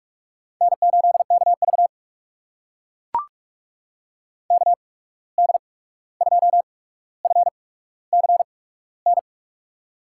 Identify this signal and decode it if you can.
Morse code, heard as a keyed tone, about 44 wpm, spaced as for 17 wpm, 700 Hz. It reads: N8KV KD2FCN
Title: USA Call Signs - 8x Character Spacing - 45wpm